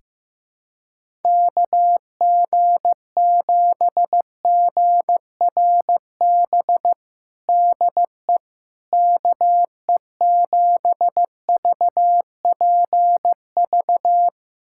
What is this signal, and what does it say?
Morse code, heard as a keyed tone, about 15 wpm, 700 Hz. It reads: KG7GRB DE KE7VPV
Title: USA Callsign DE Callsign - 6 Characters - 15wpm